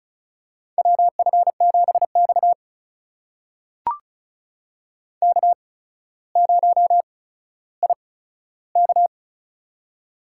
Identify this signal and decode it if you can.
Morse code, heard as a keyed tone, about 36 wpm, spaced as for 15 wpm, 700 Hz. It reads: WF7X K0IK
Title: USA Call Signs - 4 Characters - 8x Character Spacing - 35wpm